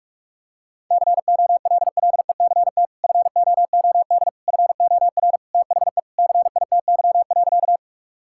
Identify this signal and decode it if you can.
Morse code, heard as a keyed tone, about 41 wpm, 700 Hz. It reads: COLLECT FOOD FOR THE CITY.
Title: Sentences from Top 1600 Words 45wpm